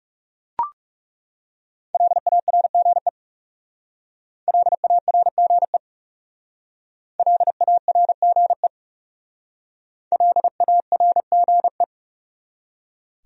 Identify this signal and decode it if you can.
Morse code, heard as a keyed tone, about 39 wpm, 700 Hz. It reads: LARGE LARGE LARGE LARGE